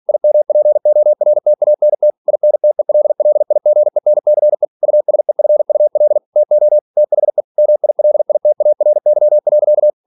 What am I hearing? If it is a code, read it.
Morse code, heard as a keyed tone, about 46 wpm, 600 Hz. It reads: IMPORTANT INTELLIGENCE USEFUL TO THE MILITARY.
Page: Never 47wpm